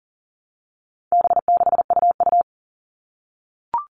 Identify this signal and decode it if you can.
Morse code, heard as a keyed tone, about 40 wpm, 700 Hz. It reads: B6UU
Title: All Cut Numbers - 4-Digit 40wpm